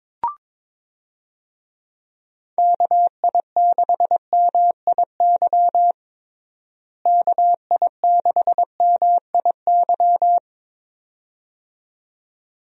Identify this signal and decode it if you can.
Morse code, heard as a keyed tone, about 22 wpm, 700 Hz. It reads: KI6MIY KI6MIY
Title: USA Call Signs - 6 Characters - Repeated 2x - 22wpm